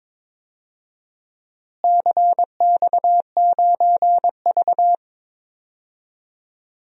Morse code, heard as a keyed tone, about 22 wpm, 700 Hz. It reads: CX9V
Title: DXCC-100 Call Signs - 4 Characters - Repeated 2x - 22wpm